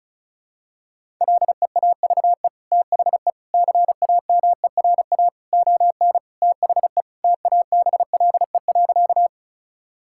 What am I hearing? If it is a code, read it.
Morse code, heard as a keyed tone, about 36 wpm, 700 Hz. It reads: LEAVE THE CAMERA ON THE TABLE.
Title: Sentences from Top 1000 Words - Review Part 5 - 35wpm